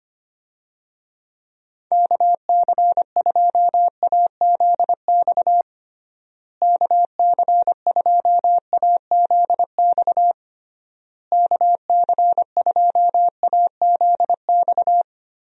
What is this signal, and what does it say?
Morse code, heard as a keyed tone, about 25 wpm, 700 Hz. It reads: KC2AZX KC2AZX KC2AZX
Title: USA Call Signs - 6 Characters - Repeated 3x - 25wpm